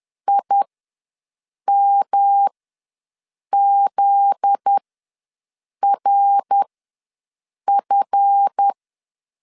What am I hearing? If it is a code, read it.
Morse code, heard as a keyed tone, about 10 wpm, spaced as for 6 wpm, 800 Hz. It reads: IMZRF
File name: LEZIONI / 7 GF9Z / 7 GF9Z ws 10wpm 3919